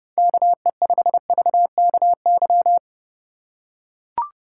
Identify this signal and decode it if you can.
Morse code, heard as a keyed tone, about 31 wpm, 700 Hz. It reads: KE5VKY